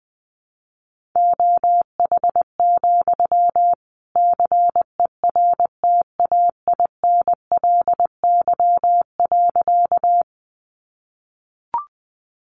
Morse code, heard as a keyed tone, about 20 wpm, 700 Hz. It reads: OH, CERTAINLY.
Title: Sentences from Top 900 Words 20wpm